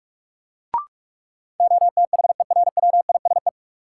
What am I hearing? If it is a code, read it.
Morse code, heard as a keyed tone, about 46 wpm, 700 Hz. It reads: OTHERWISE
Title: Sentences from Top 1200 Words - New Word Drill - 45wpm